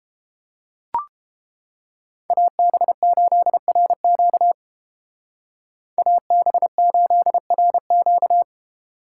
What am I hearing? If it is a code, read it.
Morse code, heard as a keyed tone, about 31 wpm, 700 Hz. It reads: AB8RQ AB8RQ